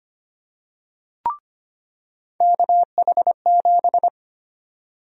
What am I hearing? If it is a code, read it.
Morse code, heard as a keyed tone, about 25 wpm, 700 Hz. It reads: KH7
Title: USA Call Sign Prefixes - 25wpm